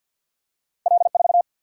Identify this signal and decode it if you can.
Morse code, heard as a keyed tone, about 47 wpm, 700 Hz. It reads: L4
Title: Single-Letter Single-Number 50wpm